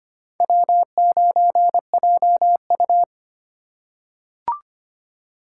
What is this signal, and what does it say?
Morse code, heard as a keyed tone, about 25 wpm, 700 Hz. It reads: W9JU
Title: USA Call Signs - 4 Characters - 4x Character Spacing - 25wpm